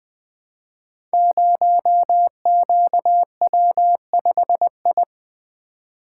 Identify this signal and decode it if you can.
Morse code, heard as a keyed tone, about 20 wpm, 700 Hz. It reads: 0QW5I
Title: Code Groups - 5 Letter:Numbers - 20wpm